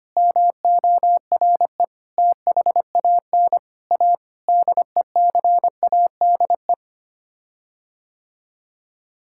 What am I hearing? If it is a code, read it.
Morse code, heard as a keyed tone, about 25 wpm, 700 Hz. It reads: MORE THAN A DECADE
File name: Sets of 9 Words - 25wpm